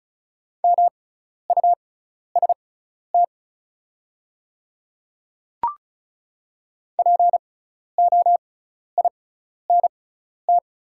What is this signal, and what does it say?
Morse code, heard as a keyed tone, about 36 wpm, spaced as for 12 wpm, 700 Hz. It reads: MUST POINT
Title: 6x Character Spacing - Top 200 Words - 35wpm